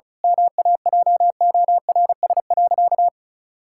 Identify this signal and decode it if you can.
Morse code, heard as a keyed tone, about 36 wpm, 700 Hz. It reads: MAJORS.
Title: Pangram Challenge v2 - 35wpm